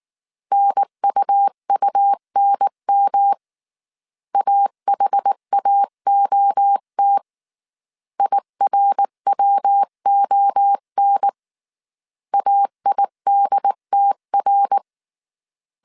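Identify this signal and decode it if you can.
Morse code, heard as a keyed tone, about 19 wpm, 800 Hz. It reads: NUUNM AHAOT IRWON AIDTR
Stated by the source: Morse code dictation at 18 wpm from the IW3SGT QRS Morse code training course, five-character groups of 300 characters.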